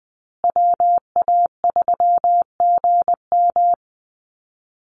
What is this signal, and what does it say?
Morse code, heard as a keyed tone, about 20 wpm, 700 Hz. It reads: WA3GM
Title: Call Signs Repeated 2x - 20wpm